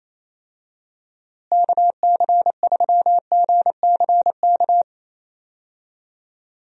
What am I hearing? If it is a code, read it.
Morse code, heard as a keyed tone, about 28 wpm, 700 Hz. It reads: KC3GCK
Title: USA Call Signs - 6 Characters - Repeated 3x - 28wpm